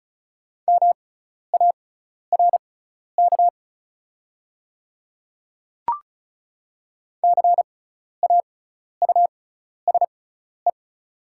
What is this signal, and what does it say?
Morse code, heard as a keyed tone, about 36 wpm, spaced as for 12 wpm, 690 Hz. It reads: MARK CAUSE
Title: 6x Character Spacing - Top 300 Words - 35wpm